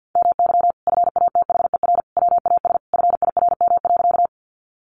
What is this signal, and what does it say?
Morse code, heard as a keyed tone, about 50 wpm, 700 Hz. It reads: MY FATHER WAS FIRM.
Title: Sentences from Top 1400 Words 50wpm